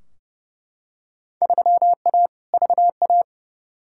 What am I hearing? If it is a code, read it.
Morse code, heard as a keyed tone, about 30 wpm, 700 Hz. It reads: 3A VA
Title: Field Day Exchanges - 30wpm